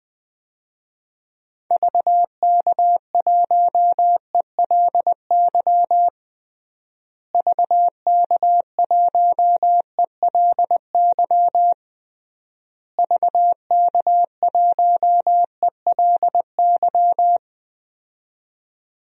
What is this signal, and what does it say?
Morse code, heard as a keyed tone, about 20 wpm, 700 Hz. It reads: VK1ELY VK1ELY VK1ELY
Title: DXCC-100 Call Signs - 6 Characters - Repeated 3x - 20wpm